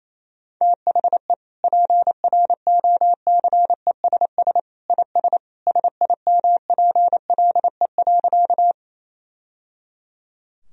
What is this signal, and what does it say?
Morse code, heard as a keyed tone, about 28 wpm, 700 Hz. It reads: THE PROCESS IS SIMPLE.
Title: Sentences from Top 1000 Words - Review Part 3 - 28wpm